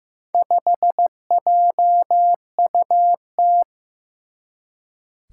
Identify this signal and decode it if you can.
Morse code, heard as a keyed tone, about 15 wpm, 700 Hz. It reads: 5JUT
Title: Four Challenging Letter-Number Combinations 15wpm